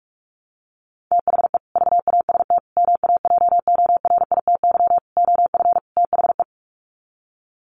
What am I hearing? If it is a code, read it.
Morse code, heard as a keyed tone, about 44 wpm, 700 Hz. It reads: THE VAST MAJORITY OF THE
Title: Sets of 5 Words Encore 45wpm